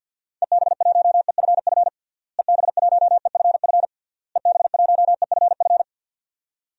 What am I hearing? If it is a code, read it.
Morse code, heard as a keyed tone, about 50 wpm, 700 Hz. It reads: EB1EFF EB1EFF EB1EFF
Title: DXCC-100 Call Signs - 6 Characters - Repeated 3x - 50wpm